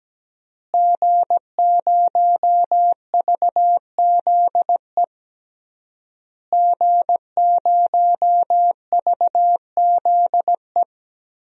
Morse code, heard as a keyed tone, about 17 wpm, 700 Hz. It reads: G0VZE G0VZE
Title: DXCC-100 Call Signs - 5 Characters - Repeated 3x - 17wpm